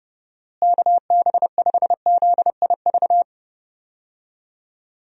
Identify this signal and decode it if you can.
Morse code, heard as a keyed tone, about 31 wpm, 700 Hz. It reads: KB5ZIV
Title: USA Call Signs - 6 Characters - Repeated 3x - 30wpm